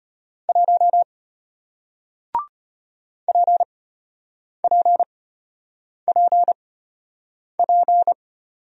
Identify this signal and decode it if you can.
Morse code, heard as a keyed tone, about 33 wpm, spaced as for 8 wpm, 700 Hz. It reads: 1 PPPP